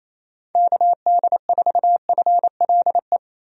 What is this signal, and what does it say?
Morse code, heard as a keyed tone, about 28 wpm, 700 Hz. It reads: KD4FLE
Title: USA Call Signs - 6 Characters - 28wpm